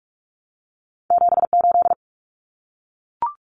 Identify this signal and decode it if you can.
Morse code, heard as a keyed tone, about 44 wpm, 700 Hz. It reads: 78
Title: Common Cut Numbers - 2-Digit 45wpm